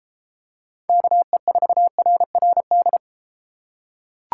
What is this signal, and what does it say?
Morse code, heard as a keyed tone, about 32 wpm, 700 Hz. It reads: KE4RRD